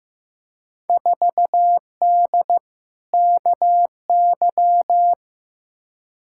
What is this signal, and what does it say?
Morse code, heard as a keyed tone, about 15 wpm, 700 Hz. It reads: 4D KY
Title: Field Day Exchanges Encore - 15wpm